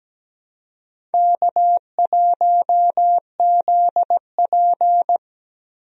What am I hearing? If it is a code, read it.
Morse code, heard as a keyed tone, about 17 wpm, 700 Hz. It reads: K1ZP
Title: USA Call Signs - 4 Characters - Repeated 3x - 17wpm